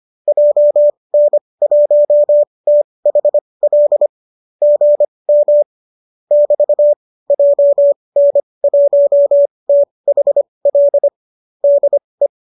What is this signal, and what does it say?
Morse code, heard as a keyed tone, about 25 wpm, spaced as for 20 wpm, 590 Hz. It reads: JN1THL GM <BT> JN1THL DE
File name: qso205 20